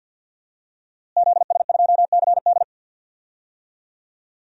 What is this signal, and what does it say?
Morse code, heard as a keyed tone, about 52 wpm, 700 Hz. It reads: ZS2CD
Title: DXCC-100 Call Signs - 5 Characters - Repeated 2x - 50wpm